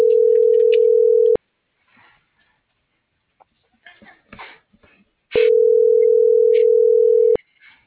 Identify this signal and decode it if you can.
Ringback tone (North America), 440 + 480 Hz.
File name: CENTRAL MULTIMODAL / out josimar.cruz